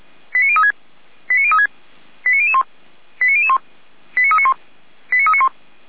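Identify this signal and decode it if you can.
Five-tone selective call (ZVEI), heard as quick runs of five tones, tones of about 80 ms. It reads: 89036 89036 89001 89001 89391 89391
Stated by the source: Received on an SDR.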